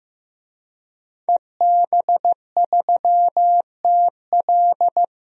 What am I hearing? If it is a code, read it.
Morse code, heard as a keyed tone, about 15 wpm, 700 Hz. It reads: EB3TL
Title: DXCC-100 Call Signs - 5 Characters - 15wpm